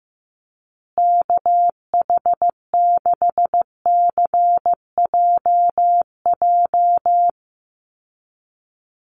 Morse code, heard as a keyed tone, about 15 wpm, 700 Hz. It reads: KH6CJJ
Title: Call Signs Repeated 2x - 15wpm